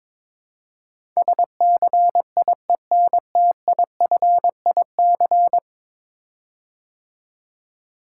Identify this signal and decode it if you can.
Morse code, heard as a keyed tone, about 22 wpm, 700 Hz. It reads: SCIENTIFIC
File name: practice / 22wpm / 10 Letter Words - First 3 Letters Given - 22wpm